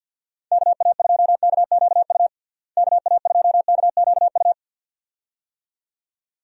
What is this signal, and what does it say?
Morse code, heard as a keyed tone, about 46 wpm, 700 Hz. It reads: KA2KQU KA2KQU
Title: USA Call Signs - 6 Characters - Repeated 2x - 50wpm